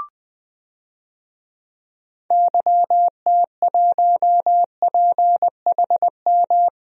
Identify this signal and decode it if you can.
Morse code, heard as a keyed tone, about 20 wpm, 700 Hz. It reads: YT1PHM
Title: DXCC-100 Call Signs - 6 Characters - 20wpm